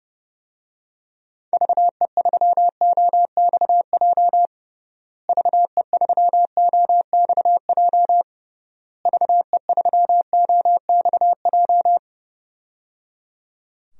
Morse code, heard as a keyed tone, about 29 wpm, 700 Hz. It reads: VE3OXJ VE3OXJ VE3OXJ
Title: DXCC-100 Call Signs - 6 Characters - Repeated 3x - 30wpm